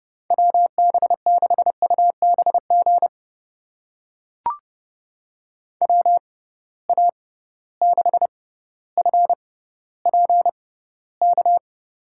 Morse code, heard as a keyed tone, about 29 wpm, spaced as for 15 wpm, 700 Hz. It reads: WB6UBG WA6FPK
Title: USA Call Signs - 6 Characters - 6x Character Spacing - 30wpm